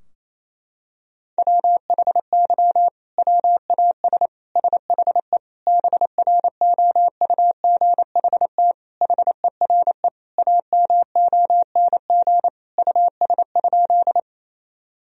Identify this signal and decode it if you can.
Morse code, heard as a keyed tone, about 28 wpm, 700 Hz. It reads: WHY WAS SHE BROUGHT HERE AMONG US?
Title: Sentences from Top 1000 Words - Review Part 3 - 28wpm